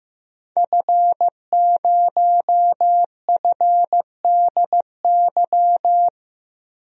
Morse code, heard as a keyed tone, about 15 wpm, 700 Hz. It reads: F0FDY
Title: DXCC-100 Call Signs - 5 Characters - 15wpm